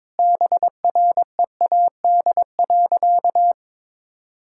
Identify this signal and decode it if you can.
Morse code, heard as a keyed tone, about 22 wpm, 690 Hz. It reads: BREAD.
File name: practice / 22wpm / Easy Sentences with Hints - 22wpm